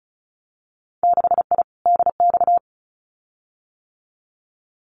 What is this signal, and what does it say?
Morse code, heard as a keyed tone, about 34 wpm, 700 Hz. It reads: 6I DX
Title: Winter Field Day Exchanges Encore 35wpm